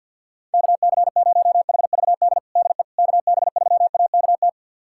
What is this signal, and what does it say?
Morse code, heard as a keyed tone, about 45 wpm, 700 Hz. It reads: KC0HVD DE KB3AKT
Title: USA Callsign DE Callsign 50wpm